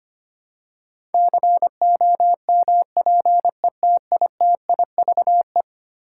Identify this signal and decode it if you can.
Morse code, heard as a keyed tone, about 25 wpm, 700 Hz. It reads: COMPETITIVE
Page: morse-code-ninja-practice-25wpm